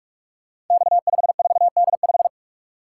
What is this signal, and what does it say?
Morse code, heard as a keyed tone, about 44 wpm, 700 Hz. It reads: X54D5
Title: Code Groups - 5 Letter:Numbers - 45wpm